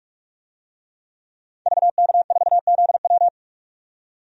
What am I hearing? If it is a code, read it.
Morse code, heard as a keyed tone, about 44 wpm, 700 Hz. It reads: VK4ZW
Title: DXCC-100 Call Signs - 5 Characters - 45wpm